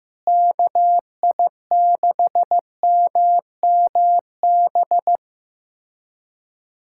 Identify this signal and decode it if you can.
Morse code, heard as a keyed tone, about 15 wpm, 700 Hz. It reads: KI6MMB
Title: USA Callsign DE Callsign 15wpm